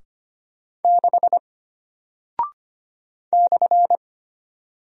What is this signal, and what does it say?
Morse code, heard as a keyed tone, about 25 wpm, 700 Hz. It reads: 6 /
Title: Single Character (Letter:Number:Punctuation:Prosign) Alt 25wpm